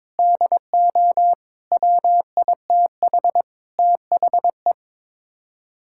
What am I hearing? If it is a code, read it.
Morse code, heard as a keyed tone, about 22 wpm, 700 Hz. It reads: DO WITH THE
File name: Sets of 9 Words - 22wpm